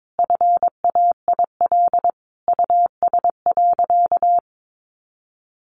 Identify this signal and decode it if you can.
Morse code, heard as a keyed tone, about 22 wpm, 700 Hz. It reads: FAIL US.
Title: Sentences from Top 1100 Words 22wpm